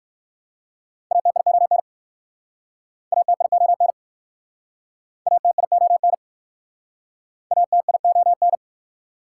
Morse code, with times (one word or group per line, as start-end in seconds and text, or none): 1.1-1.8 ATION
3.1-3.9 ATION
5.3-6.1 ATION
7.5-8.6 ATION